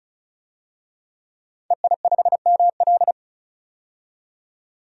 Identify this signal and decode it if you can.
Morse code, heard as a keyed tone, about 34 wpm, 700 Hz. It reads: EI5ML
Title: DXCC-100 Call Signs - 5 Characters - 35wpm